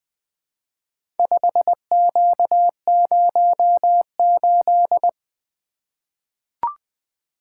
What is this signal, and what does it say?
Morse code, heard as a keyed tone, about 20 wpm, 700 Hz. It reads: 5Q08
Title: Four Letter-Number Combinations - 20wpm